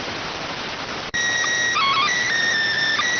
SSTV (slow-scan television), PD 290.